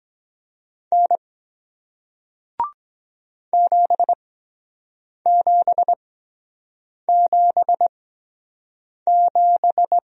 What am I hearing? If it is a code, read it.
Morse code, heard as a keyed tone, about 23 wpm, spaced as for 7 wpm, 700 Hz. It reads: N 7777